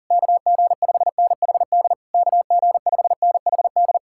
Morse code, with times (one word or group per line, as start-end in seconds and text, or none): 0.1-1.9 KG5NHD
2.1-4.0 KG5NHD